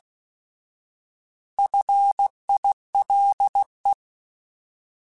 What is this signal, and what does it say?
Morse code, heard as a keyed tone, about 16 wpm, 800 Hz. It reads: FILE
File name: Alphabetical / Alphabetical mp3s / Fs @15